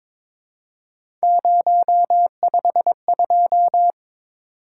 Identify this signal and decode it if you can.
Morse code, heard as a keyed tone, about 23 wpm, 700 Hz. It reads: 052
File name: practice / 22wpm / Three Letter-Number Combinations - 22wpm